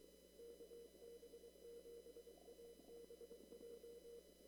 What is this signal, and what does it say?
Morse code, heard as a keyed tone, about 23 wpm, 470 Hz. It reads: KB7ABJ
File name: CW / kt5j wpx cw-op-k5pi